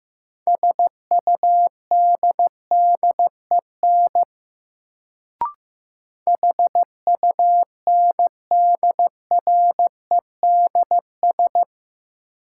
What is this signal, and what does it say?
Morse code, heard as a keyed tone, about 15 wpm, 700 Hz. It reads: SUDDEN HUNDREDS
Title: Sets of 4 Words - New Word Drill - 15wpm